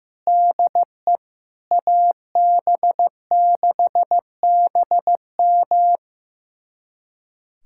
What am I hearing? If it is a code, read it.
Morse code, heard as a keyed tone, about 15 wpm, 700 Hz. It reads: DE AB6BM
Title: USA Callsign DE Callsign 15wpm